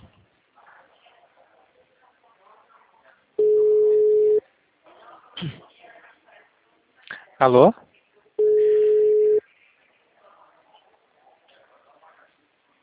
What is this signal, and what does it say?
Ringback tone (Europe, 425 Hz), with speech.
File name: out- -2007- - - .265